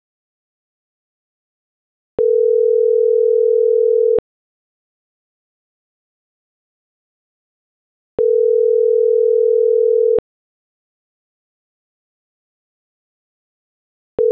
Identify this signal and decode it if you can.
Ringback tone (North America), 440 + 480 Hz.